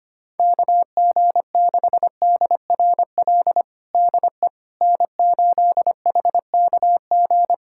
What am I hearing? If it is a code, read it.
Morse code, heard as a keyed tone, about 25 wpm, 700 Hz. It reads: KG6DRL DE N8HKG
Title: USA Callsign DE Callsign 25wpm